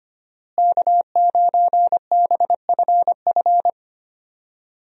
Morse code, heard as a keyed tone, about 25 wpm, 700 Hz. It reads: K9BFF
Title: USA Call Signs - 5 Characters - Repeated 2x - 25wpm